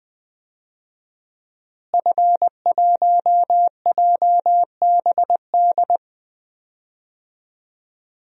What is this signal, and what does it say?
Morse code, heard as a keyed tone, about 20 wpm, 700 Hz. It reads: F1JBD
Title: DXCC-100 Call Signs - 5 Characters - Repeated 3x - 20wpm